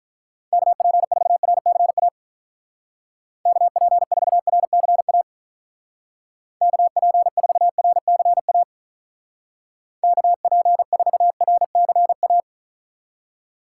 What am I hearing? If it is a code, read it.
Morse code, heard as a keyed tone, about 44 wpm, 700 Hz. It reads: KP4RCA KP4RCA KP4RCA KP4RCA
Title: Speed Racing - USA 2x3 Call Signs 35wpm